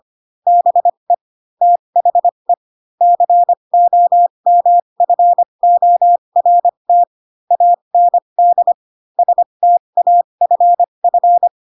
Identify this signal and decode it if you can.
Morse code, heard as a keyed tone, about 25 wpm, spaced as for 22 wpm, 700 Hz. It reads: BE THE COMFORT AND STAFF